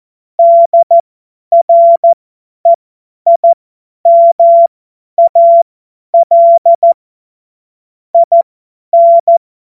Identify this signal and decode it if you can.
Morse code, heard as a keyed tone, about 14 wpm, spaced as for 10 wpm, 680 Hz. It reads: DREIMAL IN
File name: Morsekurs / Witze / Witz 06. Dezember 2022 50 F70